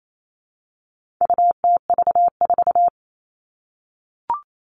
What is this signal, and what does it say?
Morse code, heard as a keyed tone, about 28 wpm, 700 Hz. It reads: UTV4
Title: All Cut Numbers - 4-Digit 28wpm